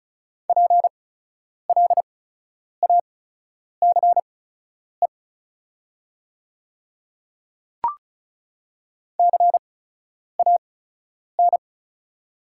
Morse code, heard as a keyed tone, about 36 wpm, spaced as for 10 wpm, 700 Hz. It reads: PLACE CAN
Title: 8x Character Spacing - Top 200 Words - 35wpm